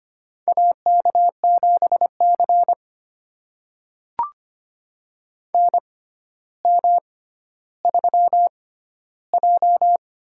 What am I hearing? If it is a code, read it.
Morse code, heard as a keyed tone, about 25 wpm, spaced as for 13 wpm, 700 Hz. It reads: AK7C NM3J
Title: USA Call Signs - 6x Character Spacing - 25wpm